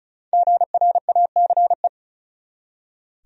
Morse code, heard as a keyed tone, about 34 wpm, 700 Hz. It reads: GRACE
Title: USA Common Names - Max 5 Letters - 35wpm